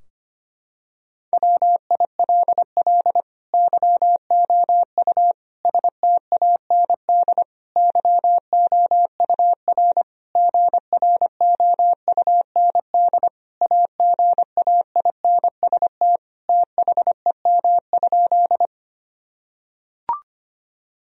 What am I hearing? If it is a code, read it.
Morse code, heard as a keyed tone, about 25 wpm, 700 Hz. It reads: WILL YOU STAND YOUR GROUND AGAINST THEM?
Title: Sentences from Top 1000 Words - Review Part 2 - 25wpm